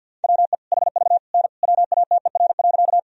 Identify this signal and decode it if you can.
Morse code, heard as a keyed tone, about 51 wpm, 700 Hz. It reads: WE HV N WATER.
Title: Sentences from Top 100 Words and Abbreviations 50wpm